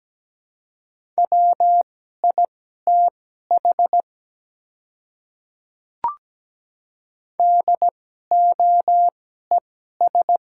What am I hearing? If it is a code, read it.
Morse code, heard as a keyed tone, about 17 wpm, spaced as for 12 wpm, 700 Hz. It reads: WITH DOES
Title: 2x Character Spacing - Top 200 Words - 17wpm